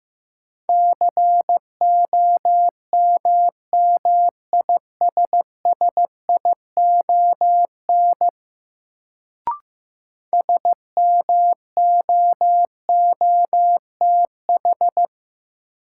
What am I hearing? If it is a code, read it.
Morse code, heard as a keyed tone, about 15 wpm, 700 Hz. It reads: COMMISSION SMOOTH
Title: Sentences from Top 1400 Words - New Word Drill Alt - 15wpm